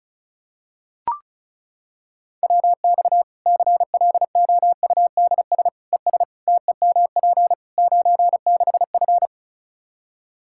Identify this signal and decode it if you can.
Morse code, heard as a keyed tone, about 36 wpm, 700 Hz. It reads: WX CLOUDS ES TEMP 96F
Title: Practice QSO - Exchange 2 - Info - 35wpm